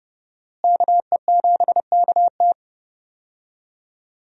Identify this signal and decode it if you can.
Morse code, heard as a keyed tone, about 30 wpm, 700 Hz. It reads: KE7KT
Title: USA Call Signs - 5 Characters - Repeated 3x - 30wpm